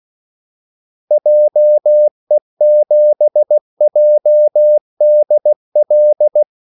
Morse code, heard as a keyed tone, about 16 wpm, 600 Hz. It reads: JE7JDL
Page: Kaldesignaler 16 wpm